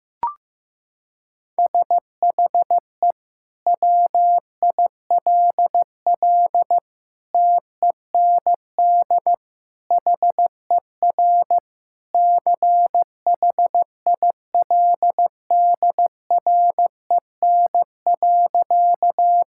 Morse code, heard as a keyed tone, about 15 wpm, 700 Hz. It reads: SHE WILL TEND HER CHILDREN.
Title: Sentences from Top 1000 Words - Review Part 5 - 15wpm